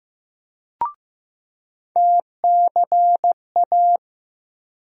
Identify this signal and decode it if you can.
Morse code, heard as a keyed tone, about 15 wpm, 700 Hz. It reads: TCA
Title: USA Call Sign Postfixes - 3 Letters 15wpm